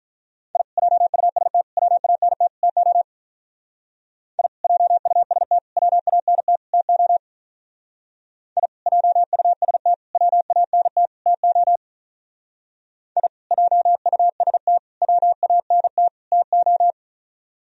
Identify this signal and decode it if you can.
Morse code, heard as a keyed tone, about 46 wpm, 700 Hz. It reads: I JUST WANT TO I JUST WANT TO I JUST WANT TO I JUST WANT TO